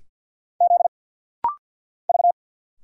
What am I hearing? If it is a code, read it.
Morse code, heard as a keyed tone, about 49 wpm, 700 Hz. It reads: Z V
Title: Single Letters - Rapid-Fire 50wpm